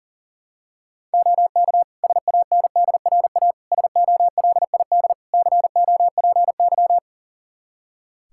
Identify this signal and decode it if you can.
Morse code, heard as a keyed tone, about 40 wpm, 700 Hz. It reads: OK SANDRA SOLID COPY
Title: Practice QSO - Exchange 3 - More Info - 40wpm